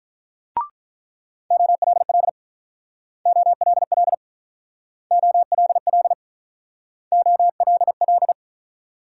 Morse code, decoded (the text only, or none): OLL OLL OLL OLL